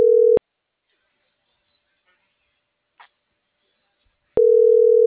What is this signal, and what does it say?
Ringback tone (North America), 440 + 480 Hz.